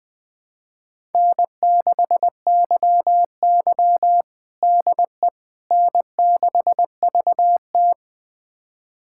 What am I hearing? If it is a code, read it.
Morse code, heard as a keyed tone, about 20 wpm, 700 Hz. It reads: N6YY DE N6VT
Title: USA Callsign DE Callsign - 4 Characters - 20wpm